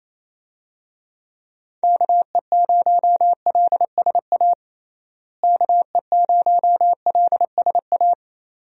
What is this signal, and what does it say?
Morse code, heard as a keyed tone, about 28 wpm, 700 Hz. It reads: KE0LSA KE0LSA
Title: USA Call Signs - 6 Characters - Repeated 2x - 28wpm